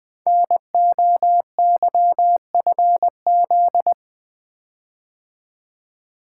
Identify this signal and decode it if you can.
Morse code, heard as a keyed tone, about 21 wpm, 700 Hz. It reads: NOYFZ
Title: Code Groups - 5 Letters - 20wpm